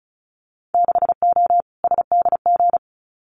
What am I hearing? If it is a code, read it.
Morse code, heard as a keyed tone, about 34 wpm, 700 Hz. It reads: 6O SDG
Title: Winter Field Day Exchanges Encore 35wpm